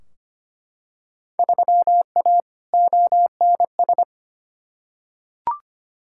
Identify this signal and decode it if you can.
Morse code, heard as a keyed tone, about 25 wpm, 700 Hz. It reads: 3A ONS
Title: Field Day Exchanges - 25wpm